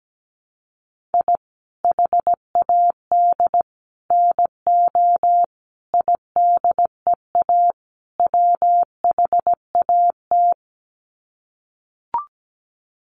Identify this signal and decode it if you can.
Morse code, heard as a keyed tone, about 17 wpm, 700 Hz. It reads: I HAD NO IDEA WHAT
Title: Sets of 5 Words Encore 17wpm